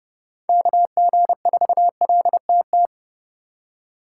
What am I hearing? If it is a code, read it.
Morse code, heard as a keyed tone, about 30 wpm, 700 Hz. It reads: KG4LTT